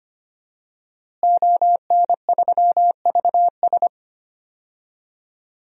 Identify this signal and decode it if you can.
Morse code, heard as a keyed tone, about 25 wpm, 700 Hz. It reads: ON3VS
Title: DXCC-100 Call Signs - 5 Characters - Repeated 3x - 25wpm